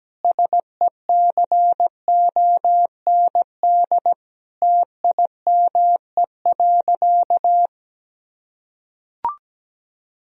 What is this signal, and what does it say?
Morse code, heard as a keyed tone, about 17 wpm, 700 Hz. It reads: SECOND TIME.
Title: Sentences from Top 1000 Words - Review Part 1 - 17wpm